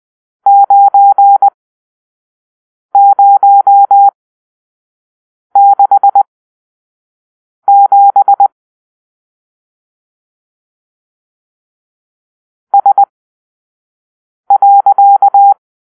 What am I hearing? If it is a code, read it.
Morse code, heard as a keyed tone, about 20 wpm, spaced as for 5 wpm, 800 Hz. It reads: 9067 S.